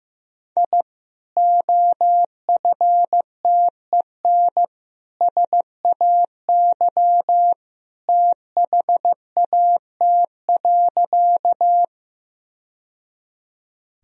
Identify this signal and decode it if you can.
Morse code, heard as a keyed tone, about 15 wpm, 700 Hz. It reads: I OFTEN SAY THAT.
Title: Sentences from Top 1000 Words - Review Part 1 - 15wpm